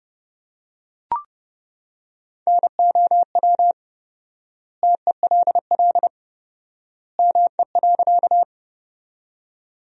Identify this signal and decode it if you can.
Morse code, heard as a keyed tone, about 30 wpm, 700 Hz. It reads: NOW TELL ME.
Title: 4x Word-Spacing - Sentences from Top 200 Words - 30wpm